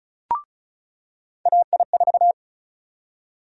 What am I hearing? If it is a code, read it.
Morse code, heard as a keyed tone, about 34 wpm, 700 Hz. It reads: AI4
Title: USA Call Sign Prefixes - 35wpm